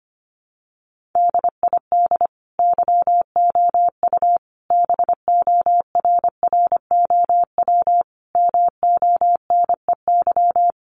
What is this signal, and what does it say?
Morse code, heard as a keyed tone, about 25 wpm, 700 Hz. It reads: DID YOU BORROW MONEY
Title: Sentences from Top 1500 Words 25wpm